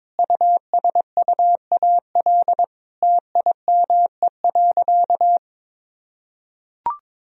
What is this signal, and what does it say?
Morse code, heard as a keyed tone, about 22 wpm, 700 Hz. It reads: USUAL TIME.
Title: Sentences from Top 400 Words and Abbreviations 22wpm